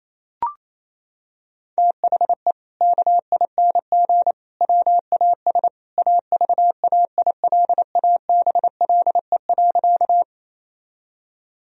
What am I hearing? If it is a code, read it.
Morse code, heard as a keyed tone, about 28 wpm, 700 Hz. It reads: THE KING WAS AVAILABLE.
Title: Sentences from Top 1000 Words - Review Part 3 - 28wpm